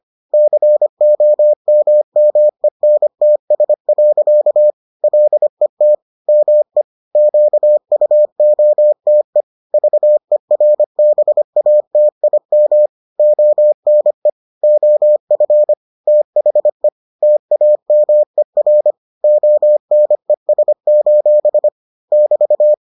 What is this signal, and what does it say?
Morse code, heard as a keyed tone, about 25 wpm, 600 Hz. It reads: COMMENTS. LET ME QUOTE VERBATIM ONE OF THE TAMER ONES: <BT>